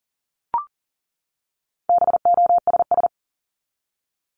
Morse code, heard as a keyed tone, about 40 wpm, 700 Hz. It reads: BOSS
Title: Top 501-750 Words 40wpm